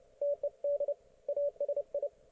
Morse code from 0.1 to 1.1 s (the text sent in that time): TED